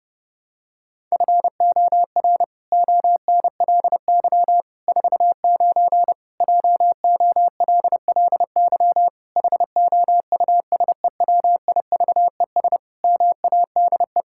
Morse code, heard as a keyed tone, about 30 wpm, 700 Hz. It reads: FOR ONLY 49 JOLLY HOUSEWIVES MADE
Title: Pangram Challenge v2 - 30wpm